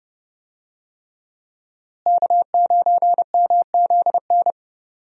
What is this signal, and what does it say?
Morse code, heard as a keyed tone, about 30 wpm, 700 Hz. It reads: K9MZN
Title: USA Call Signs - 5 Characters - Repeated 3x - 30wpm